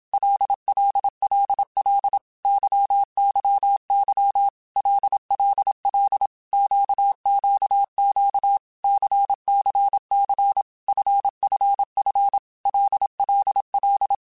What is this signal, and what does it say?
Morse code, heard as a keyed tone, about 27 wpm, 800 Hz. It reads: LLLL YYY LLL QQQ CCC FFF LLL
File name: Basic Letter Groups By7Groups / LFCYQ / LFCYQ@25